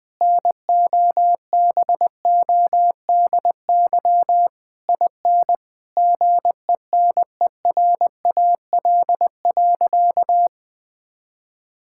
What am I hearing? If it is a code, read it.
Morse code, heard as a keyed tone, about 21 wpm, 700 Hz. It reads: NOBODY IN GENERAL.
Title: Sentences from Top 1600 Words 20wpm